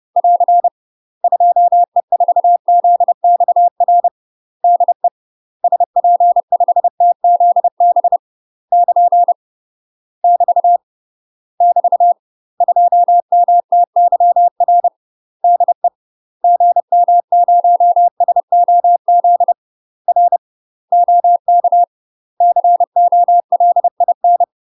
Morse code, heard as a keyed tone, about 30 wpm, 700 Hz. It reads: <AR> 2E4ZXR DE SP5TZB <KN> <BT> <BT> 2MTYR DE GM0SOZ R OK COLIN
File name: UK MASTER QSOs TRAINING / 30WPM / UK MASTER QSOs 0028